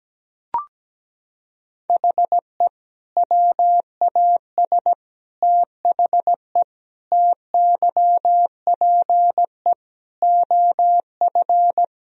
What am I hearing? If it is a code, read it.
Morse code, heard as a keyed tone, about 17 wpm, 700 Hz. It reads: HE WAS THE TYPE OF
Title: Sentences from Top 1000 Words - Review Part 3 - 17wpm